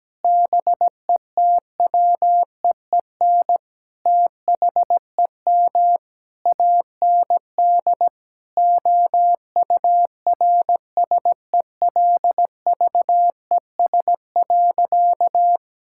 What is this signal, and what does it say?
Morse code, heard as a keyed tone, about 17 wpm, 700 Hz. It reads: BETWEEN THEM AND OURSELVES.
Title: Sentences from Top 1600 Words 17wpm